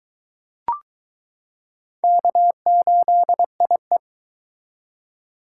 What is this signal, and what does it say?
Morse code, heard as a keyed tone, about 21 wpm, 700 Hz. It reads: K8IE